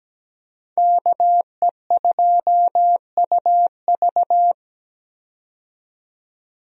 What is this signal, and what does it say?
Morse code, heard as a keyed tone, about 17 wpm, 700 Hz. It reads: KE2UV
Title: USA Call Signs - 5 Characters - 17wpm